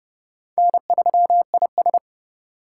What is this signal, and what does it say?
Morse code, heard as a keyed tone, about 39 wpm, 700 Hz. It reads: N3IS